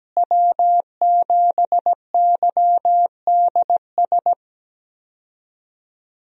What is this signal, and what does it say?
Morse code, heard as a keyed tone, about 17 wpm, 700 Hz. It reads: W7YDS
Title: USA Call Signs - 5 Characters - Repeated 3x - 17wpm